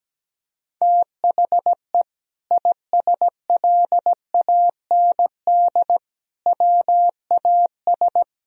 Morse code, heard as a keyed tone, about 17 wpm, 700 Hz. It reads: THE ISLAND WAS
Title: Sentences from Top 1000 Words - Review Part 3 - 17wpm